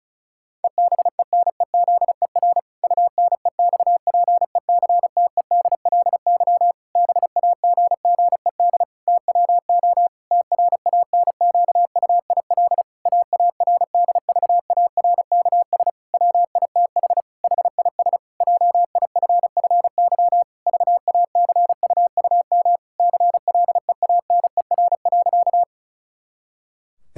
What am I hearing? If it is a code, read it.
Morse code, heard as a keyed tone, about 36 wpm, 700 Hz. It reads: EBENEZER UNEXPECTEDLY BAGGED TWO TRANQUIL AARDVARKS WITH HIS JIFFY VACUUM CLEANER.
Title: Pangram Challenge v2 - 35wpm